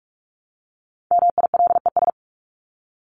Morse code, heard as a keyed tone, about 44 wpm, 700 Hz. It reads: MILES
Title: ES:S Suffix - Top 100 Words with Base Words 45wpm